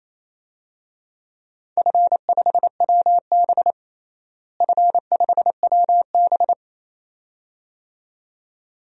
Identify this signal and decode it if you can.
Morse code, heard as a keyed tone, about 28 wpm, 700 Hz. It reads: F5WB F5WB
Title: DXCC-100 Call Signs - 4 Characters - Repeated 2x - 28wpm